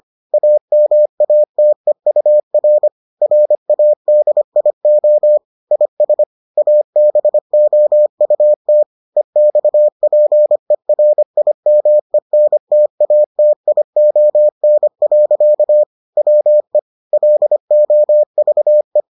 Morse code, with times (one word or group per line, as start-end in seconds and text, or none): 0.3-2.9 AMATEUR
3.2-5.4 RADIO
5.7-6.2 IS
6.6-8.8 ABOUT
9.2-15.8 EXPERIMENTATION.
16.2-16.8 WE
17.1-19.0 LOVE